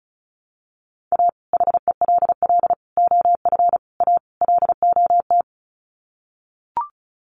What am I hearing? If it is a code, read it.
Morse code, heard as a keyed tone, about 34 wpm, 700 Hz. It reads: A HELL OF A LOT
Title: Sets of 5 Words Encore 35wpm